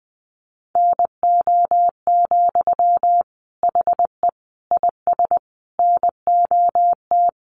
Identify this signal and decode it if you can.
Morse code, heard as a keyed tone, about 20 wpm, 700 Hz. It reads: NO, HE IS NOT
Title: Sentences from Top 1400 Words 20wpm